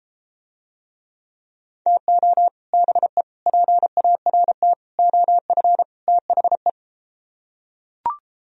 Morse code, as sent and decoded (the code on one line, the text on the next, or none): - --- / -... / .--. .- .-. - / --- ..-. / - .... .
TO BE PART OF THE